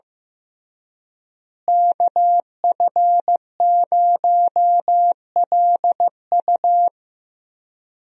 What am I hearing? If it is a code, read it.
Morse code, heard as a keyed tone, about 15 wpm, 700 Hz. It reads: KF0LU
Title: USA Call Signs - 5 Characters - Repeated 3x - 15wpm